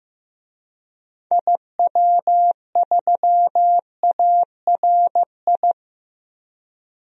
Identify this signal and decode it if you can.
Morse code, heard as a keyed tone, about 15 wpm, 700 Hz. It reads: IW3ARI
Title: DXCC-100 Call Signs - 6 Characters - 15wpm